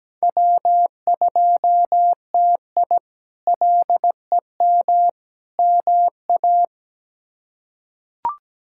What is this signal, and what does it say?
Morse code, heard as a keyed tone, about 17 wpm, 700 Hz. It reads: W2TI LEM MA
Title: K1USN SST QSO Practice - 17wpm